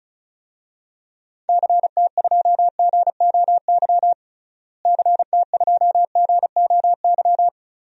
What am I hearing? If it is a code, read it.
Morse code, heard as a keyed tone, about 34 wpm, 700 Hz. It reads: CT2GOY CT2GOY
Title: DXCC-100 Call Signs - 6 Characters - Repeated 2x - 35wpm